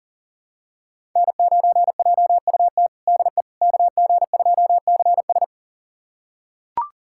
Morse code, heard as a keyed tone, about 40 wpm, 700 Hz. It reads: N9JUT DE KG2CS
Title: USA Callsign DE Callsign - 5 Characters - 40wpm